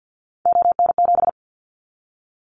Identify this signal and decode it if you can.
Morse code, heard as a keyed tone, about 50 wpm, 700 Hz. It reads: ON7
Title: Callsign Prefixes for Top 25 DXCC - 50wpm